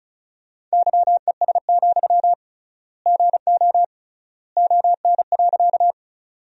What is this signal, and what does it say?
Morse code, heard as a keyed tone, about 34 wpm, 700 Hz. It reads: YES, GO ON.
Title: 3x Word-Spacing - Sentences from Top 500 Words - 35wpm